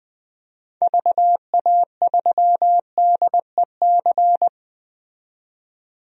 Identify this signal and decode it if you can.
Morse code, heard as a keyed tone, about 20 wpm, 700 Hz. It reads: VA3DEC
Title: DXCC-100 Call Signs - 6 Characters - 20wpm